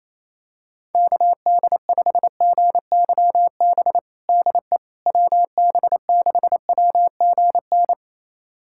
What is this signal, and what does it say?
Morse code, heard as a keyed tone, about 28 wpm, 700 Hz. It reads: KD5GYB DE WB6WGN
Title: USA Callsign DE Callsign - 6 Characters - 28wpm